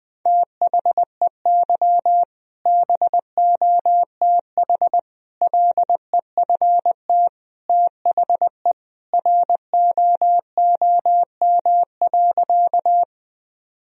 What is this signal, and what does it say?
Morse code, heard as a keyed tone, about 20 wpm, 700 Hz. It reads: THEY BOTH LEFT THE ROOM.
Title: Sentences from Top 1000 Words - Review Part 1 - 20wpm